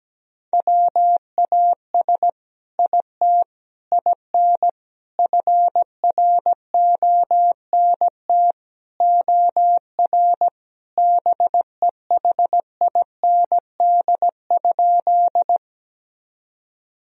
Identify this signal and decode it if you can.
Morse code, heard as a keyed tone, about 17 wpm, 700 Hz. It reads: WAS IT IN FRONT OR BEHIND?
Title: Sentences from Top 1000 Words - Review Part 2 - 17wpm